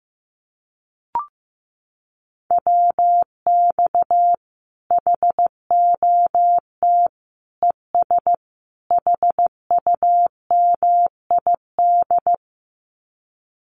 Morse code, heard as a keyed tone, about 15 wpm, 700 Hz. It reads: WX HOT ES HUMID
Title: Common QSO Phrases 15wpm